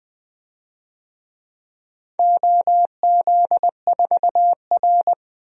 Morse code, heard as a keyed tone, about 20 wpm, 700 Hz. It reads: OZ4R
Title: DXCC-100 Call Signs - 4 Characters - 20wpm